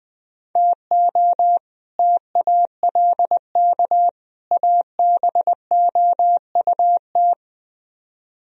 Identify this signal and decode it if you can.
Morse code, heard as a keyed tone, about 20 wpm, 700 Hz. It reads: TO TALK ABOUT